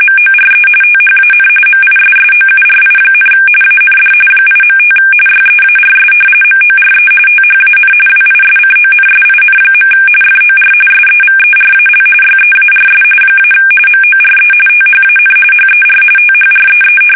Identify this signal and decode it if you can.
RTTY, about 45.45 baud. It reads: MORE THAN MISSOURI, AND 8.5 YEARS MORE THAN WISCONSIN. FOR SEVERAL YEARS - MORE THAN HER THREE
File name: fsk850 / battleshipnj